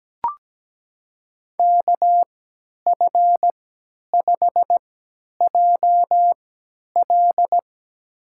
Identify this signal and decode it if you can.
Morse code, heard as a keyed tone, about 17 wpm, spaced as for 10 wpm, 700 Hz. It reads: KF5JL
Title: USA Call Signs - 5 Characters - 3x Character Spacing - 17wpm